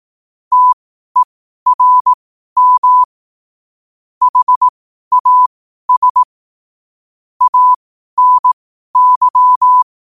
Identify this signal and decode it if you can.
Morse code, heard as a keyed tone, about 18 wpm, spaced as for 12 wpm, 1000 Hz. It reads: TERM HAS ANY
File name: qotdfiles / 12wpm